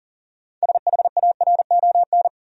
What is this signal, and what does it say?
Morse code, heard as a keyed tone, about 40 wpm, 700 Hz. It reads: SHARON
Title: USA Common Names - 40wpm